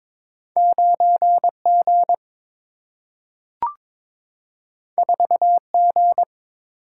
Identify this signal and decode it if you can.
Morse code, heard as a keyed tone, about 22 wpm, 700 Hz. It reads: 9 G 4 G
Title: Single-Letter Single-Number 22wpm